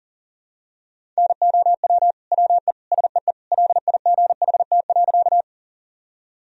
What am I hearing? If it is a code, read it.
Morse code, heard as a keyed tone, about 40 wpm, 700 Hz. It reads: NOW WE SEE LIGHT.
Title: Sentences from Top 1000 Words - Review Part 1 - 40wpm